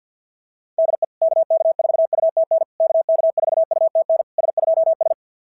Morse code, heard as a keyed tone, about 50 wpm, 660 Hz. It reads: DE KK4UTN KK4UTN S2S
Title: SOTA QSO Practice 50wpm